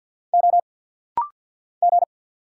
Morse code, heard as a keyed tone, about 49 wpm, 700 Hz. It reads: O G
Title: Single Letters - Rapid-Fire 50wpm